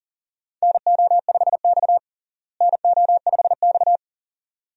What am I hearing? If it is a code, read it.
Morse code, heard as a keyed tone, about 37 wpm, 700 Hz. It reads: NO5X NO5X
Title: USA Call Signs - 4 Characters - Repeated 2x - 40wpm